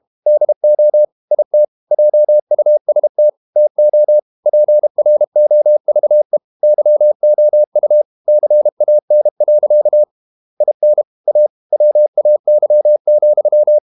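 Morse code, heard as a keyed tone, about 32 wpm, 600 Hz. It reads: DO IT JUST TO PROVE YOU CAN. IN A WAY,
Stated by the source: Morse code practice at 32 wpm from CW med Gnister, reading Ken Follett's novel Never in English.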